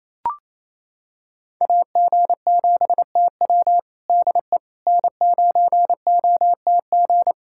Morse code, heard as a keyed tone, about 28 wpm, 700 Hz. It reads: AG7TW DE N9OTG
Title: USA Callsign DE Callsign - 5 Characters - 28wpm